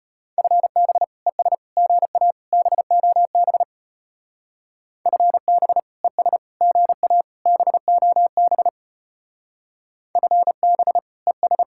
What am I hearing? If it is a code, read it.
Morse code, heard as a keyed tone, about 33 wpm, 700 Hz. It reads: FB ES GA BOB FB ES GA BOB FB ES